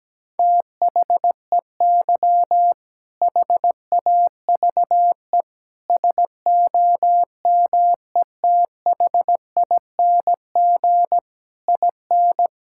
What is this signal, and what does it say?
Morse code, heard as a keyed tone, about 17 wpm, 700 Hz. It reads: THEY HAVE SOMETHING IN
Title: Sentences from Top 1000 Words - Review Part 5 - 17wpm